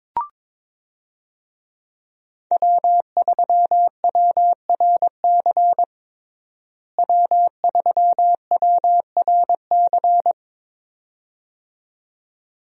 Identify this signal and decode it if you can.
Morse code, heard as a keyed tone, about 22 wpm, 700 Hz. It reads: W3WRC W3WRC
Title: USA Call Signs - 5 Characters - Repeated 2x - 22wpm